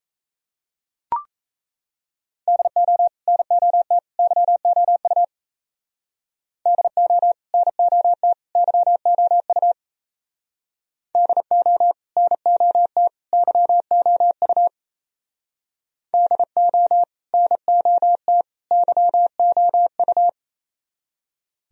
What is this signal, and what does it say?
Morse code, heard as a keyed tone, about 36 wpm, 700 Hz. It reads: DO NOT YOU DO NOT YOU DO NOT YOU DO NOT YOU